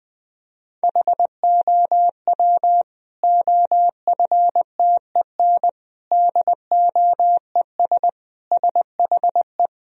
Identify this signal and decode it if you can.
Morse code, heard as a keyed tone, about 20 wpm, 700 Hz. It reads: HOW OFTEN DOES SHE
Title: Sentences from Top 1000 Words - Review Part 3 - 20wpm